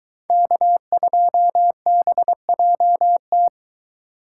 Morse code, heard as a keyed tone, about 22 wpm, 700 Hz. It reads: K2BJT